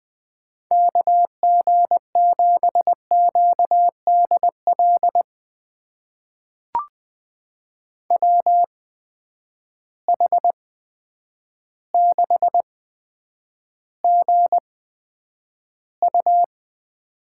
Morse code, heard as a keyed tone, about 20 wpm, spaced as for 9 wpm, 700 Hz. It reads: KG7QDL WH6GU
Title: USA Call Signs - 8x Character Spacing - 20wpm